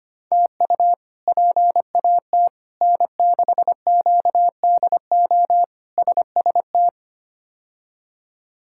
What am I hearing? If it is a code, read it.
Morse code, heard as a keyed tone, about 25 wpm, 700 Hz. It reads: TU PAT N6QDO SST
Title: K1USN SST QSO Practice - 25wpm